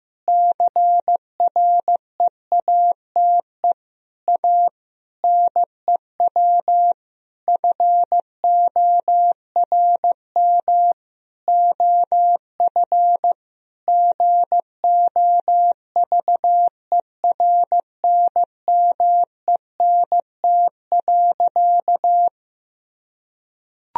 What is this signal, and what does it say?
Morse code, heard as a keyed tone, about 15 wpm, 700 Hz. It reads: CREATE A NEW FORM OF GOVERNMENT.
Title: Sentences from Top 1000 Words - Review Part 3 - 15wpm